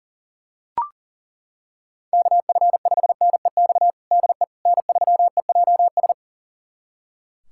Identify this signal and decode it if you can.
Morse code, heard as a keyed tone, about 40 wpm, 700 Hz. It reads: KF5NEX DE N3EJS
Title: USA Callsign DE Callsign 40wpm